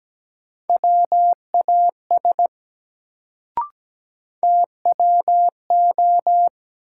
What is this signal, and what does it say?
Morse code, heard as a keyed tone, about 17 wpm, 700 Hz. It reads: WAS TWO
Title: Sentences from Top 100 Words - New Word Drill - 17wpm